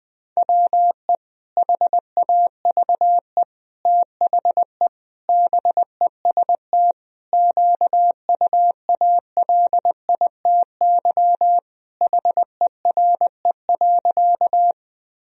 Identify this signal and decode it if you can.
Morse code, heard as a keyed tone, about 20 wpm, 700 Hz. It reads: WE HAVE THE BEST QUALITY HERE.
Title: Sentences from Top 1000 Words - Review Part 4 - 20wpm